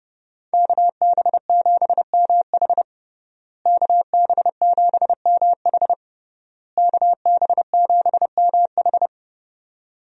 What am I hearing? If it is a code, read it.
Morse code, heard as a keyed tone, about 30 wpm, 700 Hz. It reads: KB7MH KB7MH KB7MH
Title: USA Call Signs - 5 Characters - Repeated 3x - 30wpm